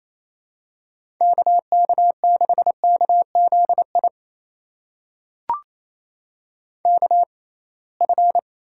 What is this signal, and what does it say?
Morse code, heard as a keyed tone, about 28 wpm, spaced as for 20 wpm, 700 Hz. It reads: K K6KZI KF
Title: USA Call Signs - 6x Character Spacing - 28wpm